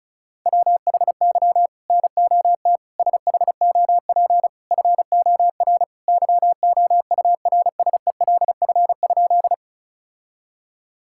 Morse code, heard as a keyed tone, about 36 wpm, 700 Hz. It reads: WHY NOT SHOP FOR YOURSELF?
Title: Sentences from Top 1600 Words 35wpm